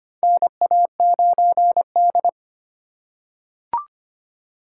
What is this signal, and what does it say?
Morse code, heard as a keyed tone, about 25 wpm, 700 Hz. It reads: NA9D
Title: USA Call Signs - 4 Characters - 2x Character Spacing - 25wpm